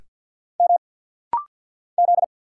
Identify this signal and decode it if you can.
Morse code, heard as a keyed tone, about 50 wpm, 700 Hz. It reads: M Z
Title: Single Letters - Rapid-Fire 50wpm